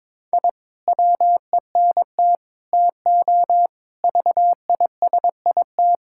Morse code, heard as a keyed tone, about 22 wpm, 700 Hz. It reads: I WENT TO VISIT
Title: Sentences from Top 1000 Words - Review Part 6 - 22wpm